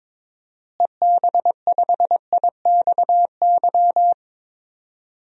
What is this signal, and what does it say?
Morse code, heard as a keyed tone, about 22 wpm, 700 Hz. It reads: EB5IXY
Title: DXCC-100 Call Signs - 6 Characters - Repeated 3x - 22wpm